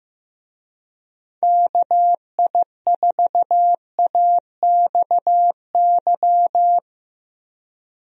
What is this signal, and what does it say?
Morse code, heard as a keyed tone, about 16 wpm, 700 Hz. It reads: KI4AXY
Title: USA Call Signs - 6 Characters - 15wpm